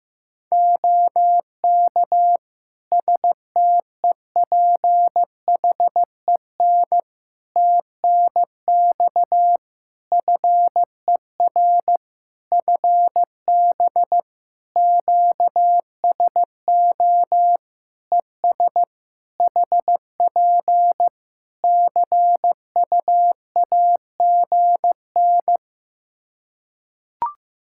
Morse code, heard as a keyed tone, about 15 wpm, 700 Hz. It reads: OK STEPHEN TNX FER FB QSO ES HP CUAGN
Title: Practice QSO - Exchange 4 - Ending - 15wpm